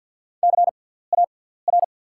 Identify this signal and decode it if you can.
Morse code, heard as a keyed tone, about 47 wpm, spaced as for 17 wpm, 690 Hz. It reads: CAR
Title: 6x Character Spacing - Top 500 Words - 50wpm